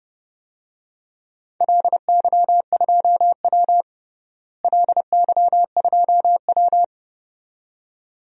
Morse code, heard as a keyed tone, about 30 wpm, 700 Hz. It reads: LY2W LY2W
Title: DXCC-100 Call Signs - 4 Characters - Repeated 2x - 30wpm